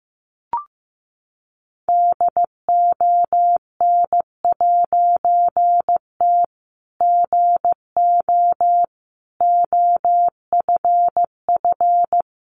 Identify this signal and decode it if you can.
Morse code, heard as a keyed tone, about 15 wpm, 700 Hz. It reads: DON'T GO OFF
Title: A Sound Of Thunder 15wpm